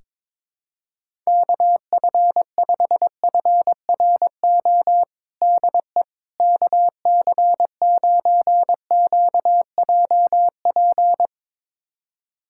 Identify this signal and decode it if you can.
Morse code, heard as a keyed tone, about 22 wpm, 700 Hz. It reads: KF5FRO DE KC9QJP
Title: USA Callsign DE Callsign - 6 Characters - 22wpm